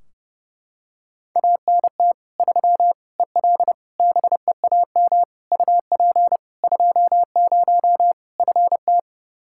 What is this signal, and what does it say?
Morse code, heard as a keyed tone, about 30 wpm, 700 Hz. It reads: ANT 3 EL BEAM UP 20 FT
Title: Practice QSO - Exchange 2 - Info - 30wpm